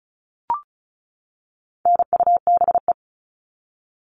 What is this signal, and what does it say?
Morse code, heard as a keyed tone, about 34 wpm, 700 Hz. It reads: NUBE
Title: All Cut Numbers - 4-Digit 35wpm